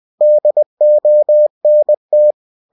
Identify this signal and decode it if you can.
Morse code, heard as a keyed tone, about 20 wpm, 600 Hz. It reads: DONT
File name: cw-books / winnie-the-pooh / winnie-20wpm-page-0040